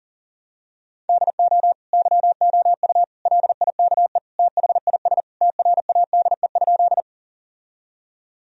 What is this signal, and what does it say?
Morse code, heard as a keyed tone, about 40 wpm, 700 Hz. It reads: DO YOU LIKE THIS TRADE?
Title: Sentences from Top 1000 Words - Review Part 4 - 40wpm